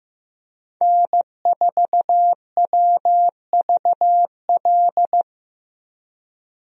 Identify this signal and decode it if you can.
Morse code, heard as a keyed tone, about 15 wpm, 700 Hz. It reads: N4WVL
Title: USA Call Signs - 5 Characters - Repeated 2x - 15wpm